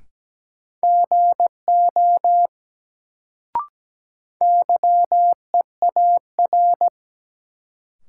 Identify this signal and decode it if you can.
Morse code, heard as a keyed tone, about 17 wpm, 700 Hz. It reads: GO YEAR
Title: Sets of 2 Words - New Word Drill - 17wpm